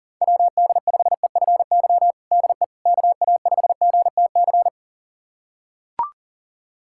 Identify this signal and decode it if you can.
Morse code, heard as a keyed tone, about 40 wpm, 700 Hz. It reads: WD5EFY DE KA5GTC
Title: USA Callsign DE Callsign - 6 Characters - 40wpm